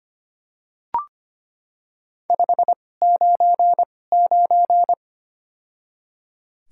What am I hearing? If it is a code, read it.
Morse code, heard as a keyed tone, about 25 wpm, spaced as for 18 wpm, 700 Hz. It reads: 599
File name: Top 90 QSO Elements - 2x Character Spacing - 25wpm